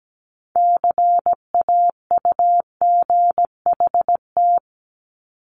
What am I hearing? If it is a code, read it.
Morse code, heard as a keyed tone, about 17 wpm, 700 Hz. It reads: CAUGHT
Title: Multi-letter Phonemes Repeated 3x 17wpm